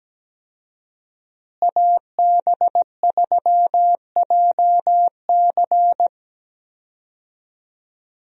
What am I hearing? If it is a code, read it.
Morse code, heard as a keyed tone, about 17 wpm, 700 Hz. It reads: AB3JC
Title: USA Call Signs - 5 Characters - Repeated 2x - 17wpm